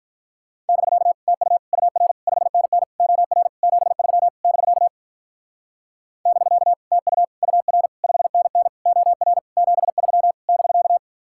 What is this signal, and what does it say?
Morse code, heard as a keyed tone, about 43 wpm, 700 Hz. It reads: <BK> TU UR 5NN OR 73 <BK> <BK> TU UR 5NN OR 73 <BK>
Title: Speed-Racing Common QSO Phrases - 35wpm